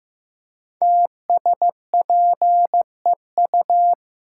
Morse code, heard as a keyed tone, about 15 wpm, 700 Hz. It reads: TSPEU
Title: Code Groups - 5 Letter:Numbers - 15wpm